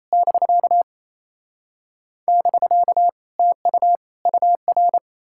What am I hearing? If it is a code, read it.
Morse code, heard as a keyed tone, about 38 wpm, spaced as for 34 wpm, 700 Hz. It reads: <BK> <BK> TU UR
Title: Speed-Racing Common QSO Phrases - 28wpm